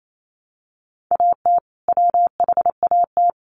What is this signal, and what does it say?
Morse code, heard as a keyed tone, about 28 wpm, 700 Hz. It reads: AT WHAT
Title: Sentences from Top 100 Words 28wpm